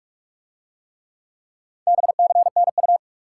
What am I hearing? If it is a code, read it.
Morse code, heard as a keyed tone, about 48 wpm, 700 Hz. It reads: BCNU